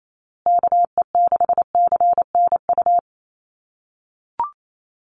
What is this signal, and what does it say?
Morse code, heard as a keyed tone, about 28 wpm, 700 Hz. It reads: KE6CNU
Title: USA Call Signs Repeated 2x 28wpm